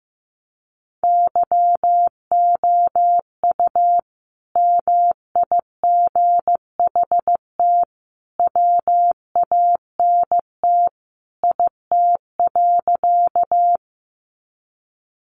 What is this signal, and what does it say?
Morse code, heard as a keyed tone, about 15 wpm, 700 Hz. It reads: YOU MIGHT WANT IT.
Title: Sentences from Top 300 Words 15wpm